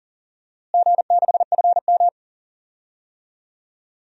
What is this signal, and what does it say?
Morse code, heard as a keyed tone, about 40 wpm, 700 Hz. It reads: G6FM
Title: DXCC-100 Call Signs - 4 Characters - Repeated 2x - 40wpm